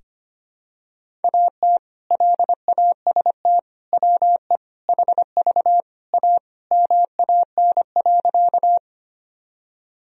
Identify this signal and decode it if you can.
Morse code, heard as a keyed tone, about 25 wpm, 700 Hz. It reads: AT LAST WE HV A MAN.
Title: Sentences from Top 300 Words and Abbreviations 25wpm